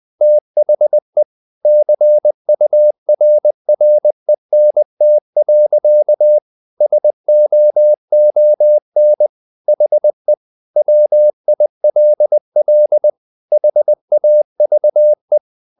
Morse code, heard as a keyed tone, about 20 wpm, 600 Hz. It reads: THE CURRENT. SOON HE WILL HAVE
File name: the old man and the sea part 2 20wpm